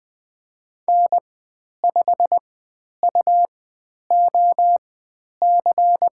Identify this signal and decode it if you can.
Morse code, heard as a keyed tone, about 20 wpm, spaced as for 10 wpm, 700 Hz. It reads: N5UOC
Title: 109 Lesson 18g - Letter m - Call Signs 20wpm